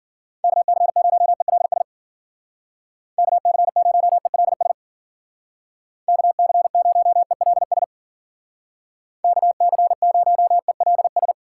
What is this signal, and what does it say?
Morse code, heard as a keyed tone, about 51 wpm, 700 Hz. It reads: KC0ELS KC0ELS KC0ELS KC0ELS